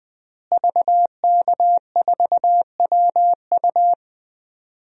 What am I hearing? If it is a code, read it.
Morse code, heard as a keyed tone, about 20 wpm, 700 Hz. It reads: VK4WU
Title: DXCC-100 Call Signs - 5 Characters - Repeated 3x - 20wpm